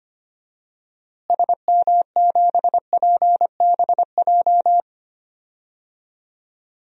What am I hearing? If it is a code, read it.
Morse code, heard as a keyed tone, about 25 wpm, 700 Hz. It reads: SM7PBJ
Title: DXCC-100 Call Signs - 6 Characters - Repeated 2x - 25wpm